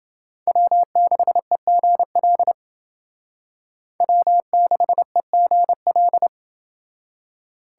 Morse code, heard as a keyed tone, about 29 wpm, 700 Hz. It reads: W6EGL W6EGL